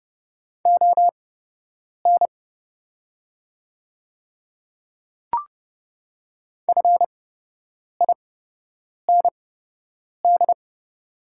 Morse code, heard as a keyed tone, about 30 wpm, spaced as for 8 wpm, 700 Hz. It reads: ON FIND
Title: 8x Character Spacing - Top 100 Words - 30wpm